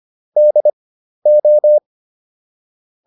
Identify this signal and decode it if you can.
Morse code, heard as a keyed tone, about 25 wpm, spaced as for 12 wpm, 600 Hz. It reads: DO